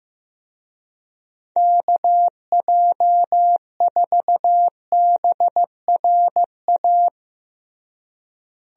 Morse code, heard as a keyed tone, about 16 wpm, 700 Hz. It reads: KJ4BRA
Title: USA Call Signs - 6 Characters - Repeated 3x - 15wpm